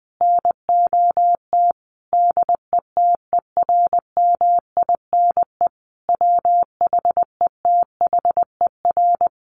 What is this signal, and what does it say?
Morse code, heard as a keyed tone, about 20 wpm, 700 Hz. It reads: NOT DETERMINE WHETHER
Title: Sentences from Top 1300 Words 20wpm